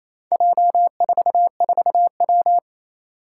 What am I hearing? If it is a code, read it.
Morse code, heard as a keyed tone, about 29 wpm, 700 Hz. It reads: J44W
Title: Four Challenging Letter-Number Combinations 28wpm